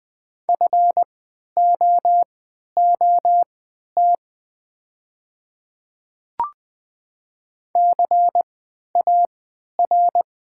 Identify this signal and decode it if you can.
Morse code, heard as a keyed tone, about 20 wpm, spaced as for 11 wpm, 700 Hz. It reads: FOOT CAR
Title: 3x Character Spacing - Top 500 Words - 20wpm